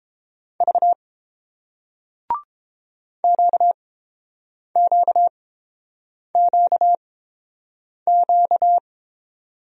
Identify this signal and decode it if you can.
Morse code, heard as a keyed tone, about 28 wpm, spaced as for 7 wpm, 700 Hz. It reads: V QQQQ